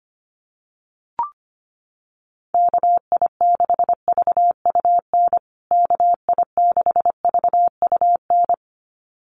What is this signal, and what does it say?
Morse code, heard as a keyed tone, about 25 wpm, 700 Hz. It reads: KI6VUN KI6VUN
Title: USA Call Signs Repeated 2x 25wpm